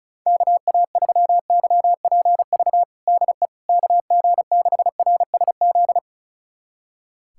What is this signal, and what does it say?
Morse code, heard as a keyed tone, about 36 wpm, 700 Hz. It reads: KA3YPV DE KG6RSZ
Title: USA Callsign DE Callsign 35wpm